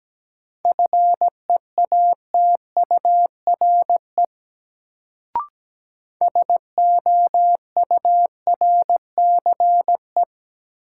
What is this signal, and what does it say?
Morse code, heard as a keyed tone, about 17 wpm, 700 Hz. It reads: FEATURE SOURCE
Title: Sentences from Top 800 Words - New Word Drill - 17wpm